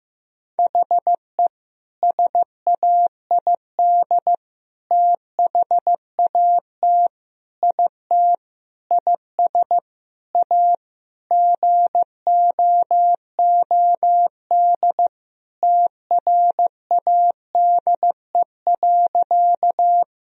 Morse code, heard as a keyed tone, about 15 wpm, 700 Hz. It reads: HE SAID THAT IT IS A GOOD TRADE.
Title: Sentences from Top 1000 Words - Review Part 4 - 15wpm